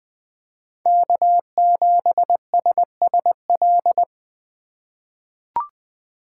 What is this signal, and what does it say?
Morse code, heard as a keyed tone, about 20 wpm, 700 Hz. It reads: K7SSL
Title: USA Call Signs - 5 Characters - 2x Character Spacing - 20wpm